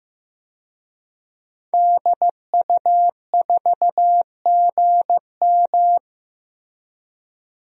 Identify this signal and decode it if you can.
Morse code, heard as a keyed tone, about 15 wpm, 700 Hz. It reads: DU4GM
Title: DXCC-100 Call Signs - 5 Characters - Repeated 3x - 15wpm